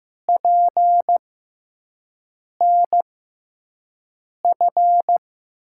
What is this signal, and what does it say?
Morse code, heard as a keyed tone, about 15 wpm, spaced as for 5 wpm, 700 Hz. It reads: PNF
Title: USA Call Signs - 5 Characters - 6x Character Spacing - 15wpm